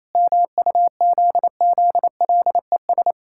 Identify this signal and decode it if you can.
Morse code, heard as a keyed tone, about 28 wpm, 700 Hz. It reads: MUZZLES
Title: Pangram Challenge v2 - 28wpm